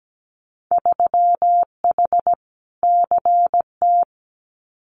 Morse code, heard as a keyed tone, about 17 wpm, 700 Hz. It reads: 3H CT
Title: Winter Field Day Exchanges Encore 17wpm